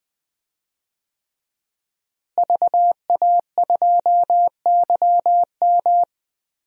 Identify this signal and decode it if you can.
Morse code, heard as a keyed tone, about 20 wpm, 700 Hz. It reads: VA2YM
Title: DXCC-100 Call Signs - 5 Characters - Repeated 3x - 20wpm